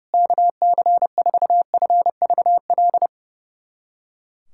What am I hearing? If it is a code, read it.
Morse code, heard as a keyed tone, about 32 wpm, 700 Hz. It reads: KC4FVL
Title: USA Callsign DE Callsign - 6 Characters - 30wpm